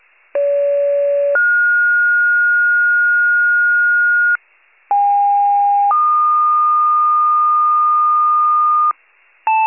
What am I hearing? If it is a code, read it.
Two-tone page, heard as a one-second tone, then a three-second tone, about 570 then 1430 Hz, about 810 then 1220 Hz, with noise. It is heard from 0.3 to 8.9 s.
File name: pager / 29 FIRE / wcalert